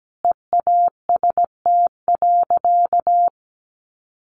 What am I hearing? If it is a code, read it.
Morse code, heard as a keyed tone, about 17 wpm, 700 Hz. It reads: EAST.
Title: Sentences from Top 1200 Words 17wpm